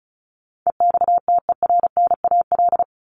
Morse code, heard as a keyed tone, about 34 wpm, 700 Hz. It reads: EXTERNAL
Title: Top 1751-2000 Words 35wpm